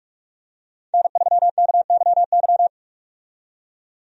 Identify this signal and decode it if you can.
Morse code, heard as a keyed tone, about 46 wpm, 700 Hz. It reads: N3KYY
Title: USA Call Signs - 5 Characters - 45wpm